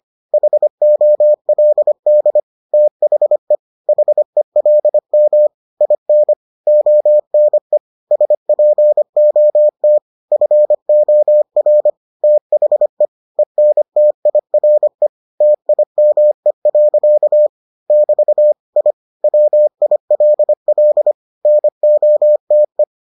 Morse code, heard as a keyed tone, about 25 wpm, 600 Hz. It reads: HOLD THE HELM IN ONE SPOT FOR THE ENTIRE TIME. <BT> I WILL NOTE